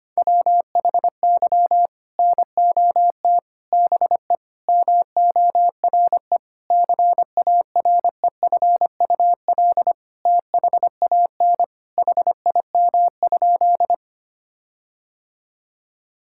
Morse code, heard as a keyed tone, about 25 wpm, 700 Hz. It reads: WHY NOT BE MORE CAREFUL THAN HIM?
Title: Sentences from Top 1600 Words 25wpm